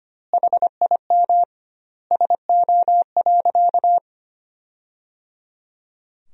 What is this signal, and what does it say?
Morse code, heard as a keyed tone, about 25 wpm, 700 Hz. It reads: HIM SO.
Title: 2x Word-Spacing - Sentences from Top 200 Words - 25wpm